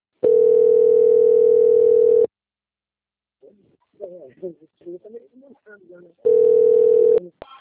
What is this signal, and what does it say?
Ringback tone (North America), 440 + 480 Hz, with speech.
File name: out- - - - - .27224